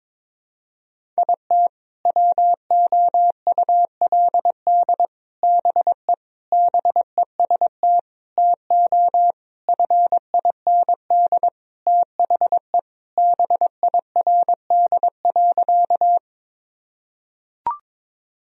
Morse code, heard as a keyed tone, about 22 wpm, 700 Hz. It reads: IT WOULD BE BEST TO FIND THE BIRD.
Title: Sentences from Top 1000 Words - Review Part 2 - 22wpm